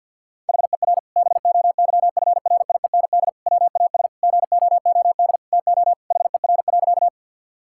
Morse code, heard as a keyed tone, about 50 wpm, 700 Hz. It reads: HER BOYFRIEND WAS GOOD TO HER.
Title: Sentences from Top 1000 Words - Review Part 5 - 50wpm